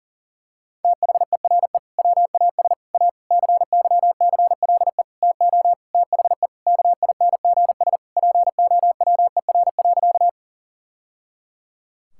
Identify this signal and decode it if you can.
Morse code, heard as a keyed tone, about 40 wpm, 700 Hz. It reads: THERE WAS A CYCLE TO THE KINGS POWER.
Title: Sentences from Top 1000 Words - Review Part 6 - 40wpm